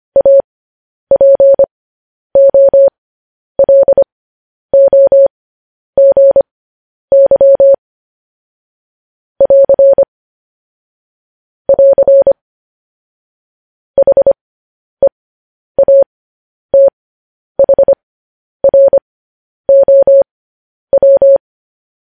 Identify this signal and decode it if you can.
Morse code, heard as a keyed tone, about 25 wpm, spaced as for 10 wpm, 550 Hz. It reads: APOLOGY <AR> <AR> HEATHROW
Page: news headlines in morse code at 10 wpm